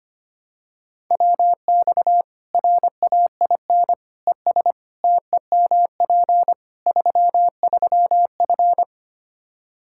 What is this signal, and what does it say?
Morse code, heard as a keyed tone, about 25 wpm, 700 Hz. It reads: WX RAIN ES TEMP 33F
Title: Practice QSO - Exchange 2 - Info - 25wpm